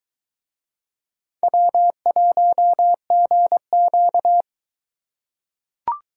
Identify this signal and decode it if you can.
Morse code, heard as a keyed tone, about 21 wpm, 700 Hz. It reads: W1GQ